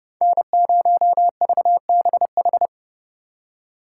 Morse code, heard as a keyed tone, about 30 wpm, 700 Hz. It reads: N0VBH